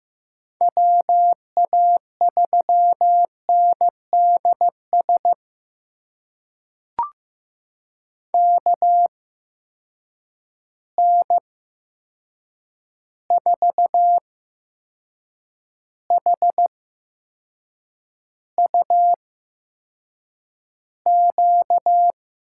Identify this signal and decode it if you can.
Morse code, heard as a keyed tone, about 15 wpm, spaced as for 6 wpm, 700 Hz. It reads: WA3NDS KN4HUQ
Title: USA Call Signs - 6 Characters - 8x Character Spacing - 15wpm